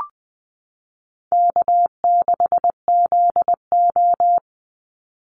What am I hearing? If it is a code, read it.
Morse code, heard as a keyed tone, about 20 wpm, 700 Hz. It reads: K6ZO
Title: Easy Call Signs - 20wpm